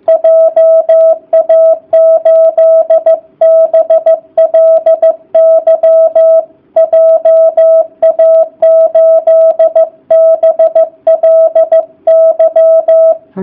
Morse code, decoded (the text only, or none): JA8BLY JA8BLY